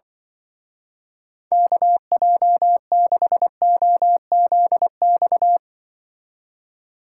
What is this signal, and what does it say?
Morse code, heard as a keyed tone, about 24 wpm, 700 Hz. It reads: KJ6OZX
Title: Speed Racing - USA 2x3 Call Signs 20wpm